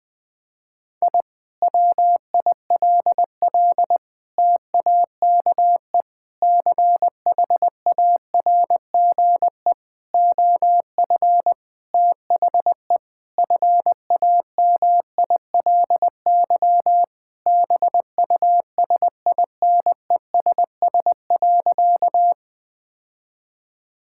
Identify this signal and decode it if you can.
Morse code, heard as a keyed tone, about 20 wpm, 700 Hz. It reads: I WILL TAKE CHARGE OF THE FAMILY BUSINESS.
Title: Sentences from Top 1000 Words - Review Part 6 - 20wpm